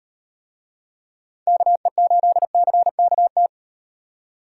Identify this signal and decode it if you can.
Morse code, heard as a keyed tone, about 38 wpm, 700 Hz. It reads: KE8CKT